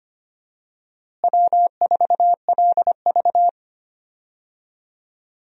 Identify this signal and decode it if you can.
Morse code, heard as a keyed tone, about 25 wpm, 700 Hz. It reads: W4LV
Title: USA Call Signs - 4 Characters - Repeated 2x - 25wpm